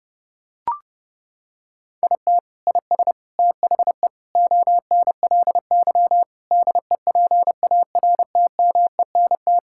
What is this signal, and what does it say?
Morse code, heard as a keyed tone, about 30 wpm, 700 Hz. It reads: IT IS THE ONLY DEPARTMENT
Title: Sentences from Top 1000 Words - Review Part 5 - 30wpm